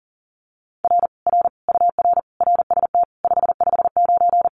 Morse code, heard as a keyed tone, about 40 wpm, 700 Hz. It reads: R R UR RST 559
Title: Common QSO Phrases Encore 40wpm